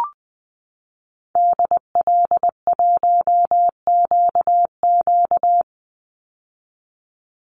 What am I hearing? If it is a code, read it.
Morse code, heard as a keyed tone, about 20 wpm, 700 Hz. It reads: DL1QQ
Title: Call Signs - 20wpm